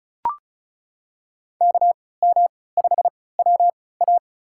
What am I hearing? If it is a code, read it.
Morse code, heard as a keyed tone, about 36 wpm, spaced as for 20 wpm, 700 Hz. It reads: KM5WA
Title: USA Call Signs - 5 Characters - 3x Character Spacing - 35wpm